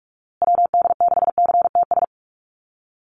Morse code, heard as a keyed tone, about 44 wpm, 700 Hz. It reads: WD6CTS
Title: USA Call Signs Repeated 3x 45wpm